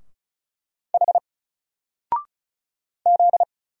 Morse code, heard as a keyed tone, about 36 wpm, 700 Hz. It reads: H Z
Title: Single Letter-Number Alt 35wpm